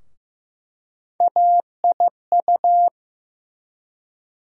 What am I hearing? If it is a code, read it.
Morse code, heard as a keyed tone, about 15 wpm, 700 Hz. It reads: AIU
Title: USA Call Sign Postfixes - 3 Letters 15wpm